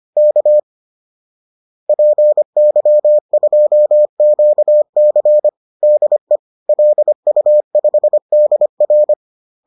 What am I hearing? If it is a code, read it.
Morse code, heard as a keyed tone, about 25 wpm, 600 Hz. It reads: K PY2QC DE LU5DR